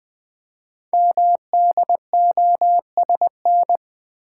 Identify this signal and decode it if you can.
Morse code, heard as a keyed tone, about 20 wpm, 700 Hz. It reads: MDOSN
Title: Code Groups - 5 Letter:Numbers - 20wpm